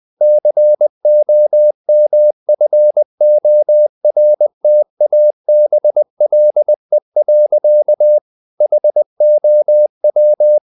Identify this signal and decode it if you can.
Morse code, heard as a keyed tone, about 20 wpm, 600 Hz. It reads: COMFORTABLE. HOW
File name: the old man and the sea / the old man and the sea part 1 20wpm